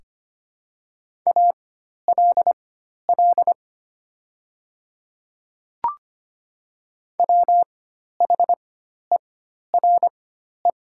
Morse code, heard as a keyed tone, about 25 wpm, spaced as for 12 wpm, 700 Hz. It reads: ALL WHERE
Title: 4x Character Spacing - Top 300 Words - 25wpm